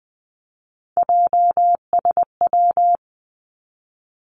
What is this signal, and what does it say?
Morse code, heard as a keyed tone, about 20 wpm, 700 Hz. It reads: JSW
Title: Three Challenging Letter-Number Combinations 20wpm